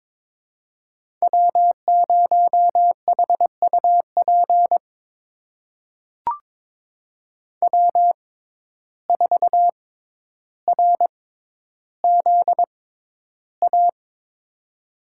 Morse code, heard as a keyed tone, about 22 wpm, spaced as for 11 wpm, 700 Hz. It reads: W0HUP W4RZA
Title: USA Call Signs - 5 Characters - 6x Character Spacing - 22wpm